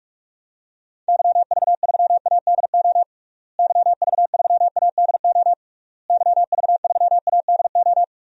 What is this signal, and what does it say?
Morse code, heard as a keyed tone, about 46 wpm, 700 Hz. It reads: YV3ADO YV3ADO YV3ADO
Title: DXCC-100 Call Signs - 6 Characters - Repeated 3x - 45wpm